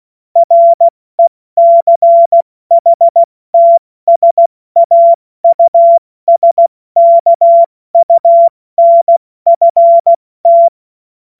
Morse code, heard as a keyed tone, about 16 wpm, 680 Hz. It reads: RECHTSAUSKUNFT
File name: Morsekurs / Witze / Witz 04. Oktober 2022 70 F80